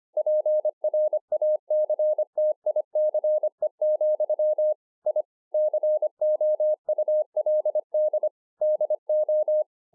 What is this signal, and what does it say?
Morse code, heard as a keyed tone, about 25 wpm, 610 Hz. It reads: PRACTICE, I COULD DO
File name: 2011-04-mp3 / THIS IS NOT MY SHACK 2 25WPM